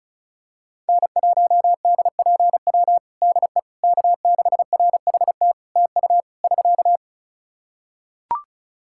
Morse code, heard as a keyed tone, about 34 wpm, 700 Hz. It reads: N1DPW DE K6RHT TU <SK>
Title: Practice QSO - Exchange 4 - Ending - 35wpm